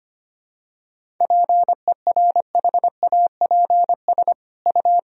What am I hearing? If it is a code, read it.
Morse code, heard as a keyed tone, about 25 wpm, 700 Hz. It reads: PERHAPS U
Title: Sentences from Top 500 Words and Abbreviations 25wpm